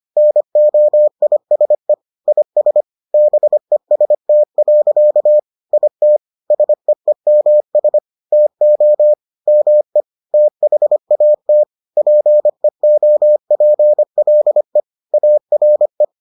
Morse code, heard as a keyed tone, about 25 wpm, 600 Hz. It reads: NOISE IS BEST. IT SEEMS TO ME THAT PEOPLE ARE